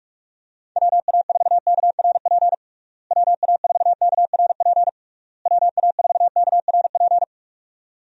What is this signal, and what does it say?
Morse code, heard as a keyed tone, about 47 wpm, 700 Hz. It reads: WA4KRP WA4KRP WA4KRP
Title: USA Call Signs - 6 Characters - Repeated 3x - 45wpm